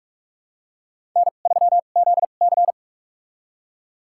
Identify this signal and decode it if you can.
Morse code, heard as a keyed tone, about 44 wpm, spaced as for 32 wpm, 700 Hz. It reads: N3ZC
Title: USA Call Signs - 4 Characters - 2x Character Spacing - 45wpm